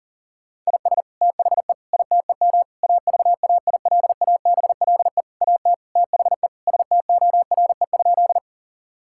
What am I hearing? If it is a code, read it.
Morse code, heard as a keyed tone, about 37 wpm, 700 Hz. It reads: IS THE ITEM AVAILABLE AT THE STORE?
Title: Sentences from Top 1000 Words - Review Part 6 - 40wpm